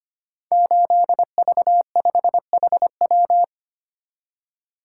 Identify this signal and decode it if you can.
Morse code, heard as a keyed tone, about 25 wpm, 700 Hz. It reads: 8V5HW
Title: Code Groups - 5 Letter:Numbers - 25wpm